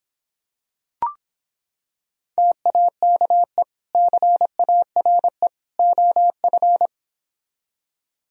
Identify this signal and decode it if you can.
Morse code, heard as a keyed tone, about 26 wpm, 700 Hz. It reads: TAKE CARE OF